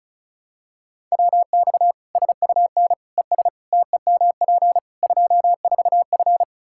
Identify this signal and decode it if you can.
Morse code, heard as a keyed tone, about 36 wpm, 700 Hz. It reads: WX SUN ES TEMP 24F
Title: Practice QSO - Exchange 2 - Info - 35wpm